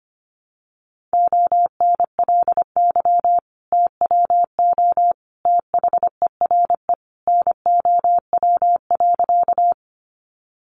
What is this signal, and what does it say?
Morse code, heard as a keyed tone, about 25 wpm, 700 Hz. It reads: ONLY TWO THERE NOW.
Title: Sentences from Top 200 Words 25wpm